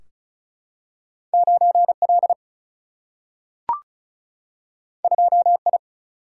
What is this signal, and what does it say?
Morse code, heard as a keyed tone, about 36 wpm, 700 Hz. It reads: 9L 2I
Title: Single-Letter Single-Number 35wpm